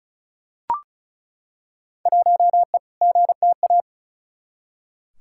Morse code, heard as a keyed tone, about 34 wpm, 700 Hz. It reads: 1E GTA
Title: Field Day Exchanges - 35wpm